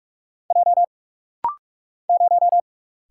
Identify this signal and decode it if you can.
Morse code, heard as a keyed tone, about 43 wpm, 700 Hz. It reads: J 0
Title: Single Letter-Number - Rapid-Fire 45wpm